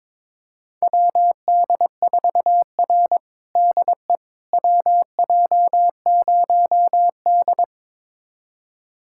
Morse code, heard as a keyed tone, about 22 wpm, 700 Hz. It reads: WD4R DE WJ0D
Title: USA Callsign DE Callsign - 4 Characters - 22wpm